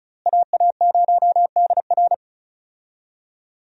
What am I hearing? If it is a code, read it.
Morse code, heard as a keyed tone, about 36 wpm, 700 Hz. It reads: AA0DR
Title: USA Call Signs - 5 Characters - Repeated 2x - 35wpm